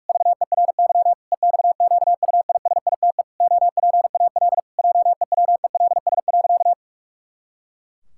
Morse code, heard as a keyed tone, about 41 wpm, 700 Hz. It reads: VERY EXQUISITE OPAL JEWELS.
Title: Pangram Challenge v2 - 45wpm